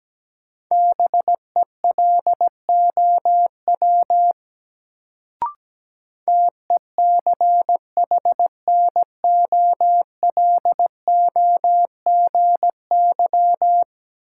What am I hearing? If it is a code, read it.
Morse code, heard as a keyed tone, about 17 wpm, 700 Hz. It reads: BELOW TECHNOLOGY
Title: Sentences from Top 1000 Words - Review Part 5 - New Word Drill - 17wpm